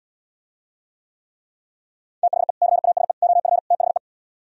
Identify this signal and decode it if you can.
Morse code, heard as a keyed tone, about 77 wpm, 700 Hz. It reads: THE CASE OF THE